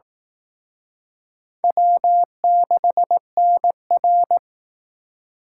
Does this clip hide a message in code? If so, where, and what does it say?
Yes: Morse code from 1.6 to 4.4 s, heard as a keyed tone, about 20 wpm, 700 Hz. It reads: W6NR